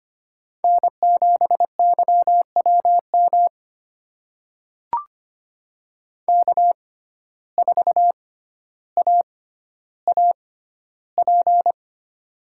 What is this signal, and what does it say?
Morse code, heard as a keyed tone, about 25 wpm, spaced as for 13 wpm, 700 Hz. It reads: N7YWM K4AAP
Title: USA Call Signs - 5 Characters - 6x Character Spacing - 25wpm